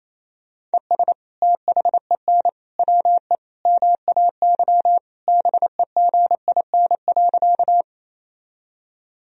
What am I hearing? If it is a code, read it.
Morse code, heard as a keyed tone, about 27 wpm, 700 Hz. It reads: ES THEN WE MAY BEGIN.
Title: Sentences from Top 300 Words and Abbreviations 28wpm